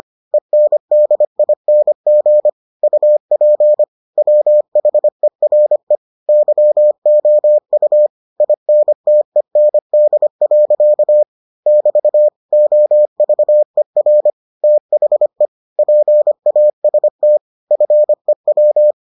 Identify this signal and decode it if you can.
Morse code, heard as a keyed tone, about 25 wpm, 600 Hz. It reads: ENDING UP WHERE YOU INTEND. <BT> OVER THE PAST FEW